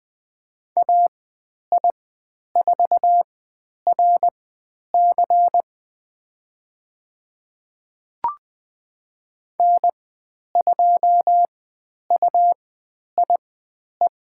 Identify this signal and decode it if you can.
Morse code, heard as a keyed tone, about 20 wpm, spaced as for 10 wpm, 700 Hz. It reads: AI4RC N2UIE
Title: 109 Lesson 18g - Letter m - Call Signs 20wpm